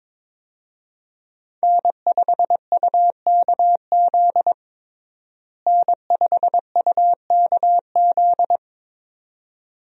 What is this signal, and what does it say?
Morse code, heard as a keyed tone, about 22 wpm, 700 Hz. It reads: N5UKZ N5UKZ
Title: USA Call Signs - 5 Characters - Repeated 2x - 22wpm